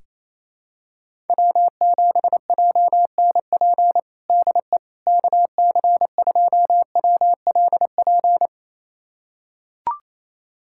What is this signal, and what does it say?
Morse code, heard as a keyed tone, about 28 wpm, 700 Hz. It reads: W7JNP DE KC2WLP
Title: USA Callsign DE Callsign 28wpm